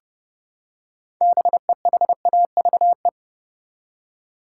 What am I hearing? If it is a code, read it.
Morse code, heard as a keyed tone, about 30 wpm, 700 Hz. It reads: BEHAVE
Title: Six Letter Words - 30wpm